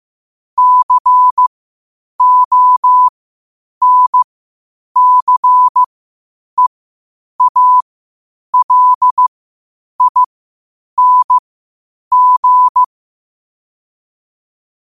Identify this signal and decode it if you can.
Morse code, heard as a keyed tone, about 15 wpm, spaced as for 8 wpm, 1000 Hz. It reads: CONCEALING
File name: qotdfiles / 8wpm